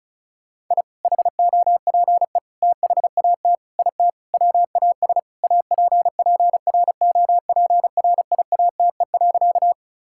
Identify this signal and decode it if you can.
Morse code, heard as a keyed tone, about 36 wpm, 700 Hz. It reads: I HOPE THAT IT WAS APPROPRIATE.
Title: Sentences from Top 1000 Words - Review Part 5 - 35wpm